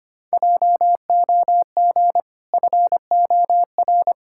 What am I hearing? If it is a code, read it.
Morse code, heard as a keyed tone, about 25 wpm, 700 Hz. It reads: JOG FOR
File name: Easy Sentences with Hints - 25wpm